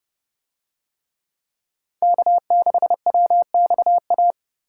Morse code, heard as a keyed tone, about 29 wpm, 700 Hz. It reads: K6WXA